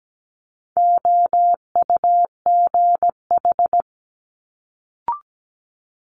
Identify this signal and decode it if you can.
Morse code, heard as a keyed tone, about 17 wpm, 700 Hz. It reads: OUGH
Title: Multi-letter Phonemes Quick Fire 17wpm